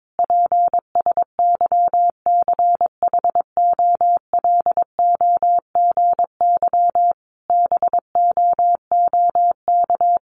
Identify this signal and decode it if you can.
Morse code, heard as a keyed tone, about 22 wpm, 700 Hz. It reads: PSYCHOLOGY BOOK
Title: Sentences from Top 1400 Words 22wpm